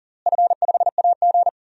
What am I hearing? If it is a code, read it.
Morse code, heard as a keyed tone, about 40 wpm, 700 Hz. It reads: F5AG
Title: DXCC-100 Call Signs - 4 Characters - Repeated 3x - 40wpm